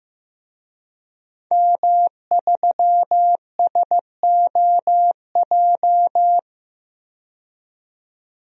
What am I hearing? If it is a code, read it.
Morse code, heard as a keyed tone, about 15 wpm, 700 Hz. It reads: M3SOJ
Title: DXCC-100 Call Signs - 5 Characters - 15wpm